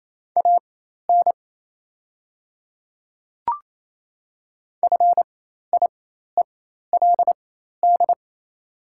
Morse code, heard as a keyed tone, about 28 wpm, spaced as for 13 wpm, 690 Hz. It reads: AN FIELD
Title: 4x Character Spacing - Top 500 Words - 28wpm